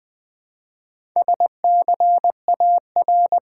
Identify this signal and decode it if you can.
Morse code, heard as a keyed tone, about 20 wpm, 700 Hz. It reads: SCAR
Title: Build A Word - 1 Letter At A Time - 3 to 5-Letter Words - 20wpm